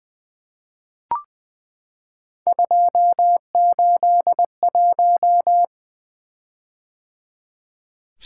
Morse code, heard as a keyed tone, about 21 wpm, 700 Hz. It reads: 281
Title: Three Letter-Number Combinations - 20wpm